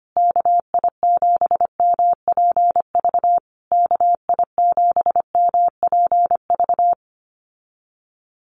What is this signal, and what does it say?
Morse code, heard as a keyed tone, about 25 wpm, 700 Hz. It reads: KI7MPV KI7MPV
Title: USA Call Signs Repeated 3x 25wpm